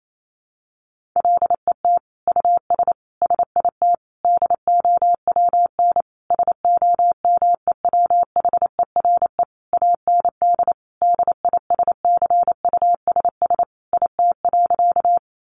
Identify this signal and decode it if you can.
Morse code, heard as a keyed tone, about 28 wpm, 700 Hz. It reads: LET US SIT DOWN SOMEWHERE AND DISCUSS IT.
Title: Sentences from Top 1400 Words 28wpm